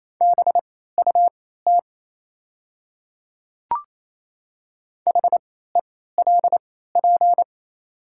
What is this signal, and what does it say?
Morse code, heard as a keyed tone, about 28 wpm, spaced as for 16 wpm, 700 Hz. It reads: BUT HELP
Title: 3x Character Spacing - Top 200 Words - 28wpm